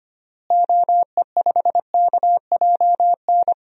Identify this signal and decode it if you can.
Morse code, heard as a keyed tone, about 25 wpm, 700 Hz. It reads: OE5KJN
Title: DXCC-100 Call Signs - 6 Characters - Repeated 3x - 25wpm